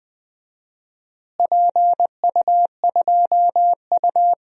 Morse code, heard as a keyed tone, about 20 wpm, 700 Hz. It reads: PU2U
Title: DXCC-100 Call Signs - 4 Characters - Repeated 3x - 20wpm